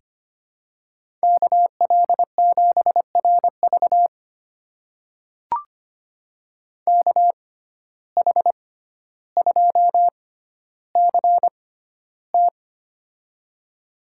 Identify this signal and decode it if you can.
Morse code, heard as a keyed tone, about 25 wpm, spaced as for 13 wpm, 700 Hz. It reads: KL7RV KH2CT
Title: USA Call Signs - 5 Characters - 6x Character Spacing - 25wpm